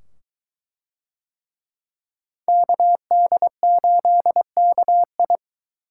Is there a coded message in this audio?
Yes: Morse code from 2.5 to 5.4 s, heard as a keyed tone, about 22 wpm, 700 Hz. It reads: KD8KI